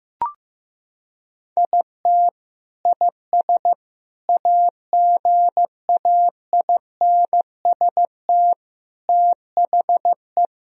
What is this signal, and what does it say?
Morse code, heard as a keyed tone, about 15 wpm, 700 Hz. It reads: IT IS AGAINST THE
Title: Sentences from Top 1000 Words - Review Part 3 - 15wpm